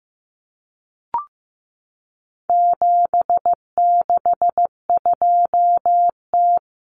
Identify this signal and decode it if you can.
Morse code, heard as a keyed tone, about 15 wpm, 700 Hz. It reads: 762T
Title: Common Cut Numbers - 4-Digit 15wpm